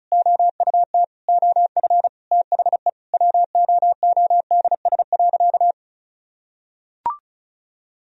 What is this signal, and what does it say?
Morse code, heard as a keyed tone, about 36 wpm, 700 Hz. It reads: OUT OF THE WOODS.
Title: Sentences from Top 1000 Words - Review Part 2 - 35wpm